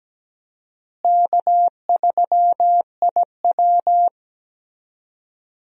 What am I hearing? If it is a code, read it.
Morse code, heard as a keyed tone, about 17 wpm, 700 Hz. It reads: K3IW
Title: USA Call Signs - 4 Characters - Repeated 3x - 17wpm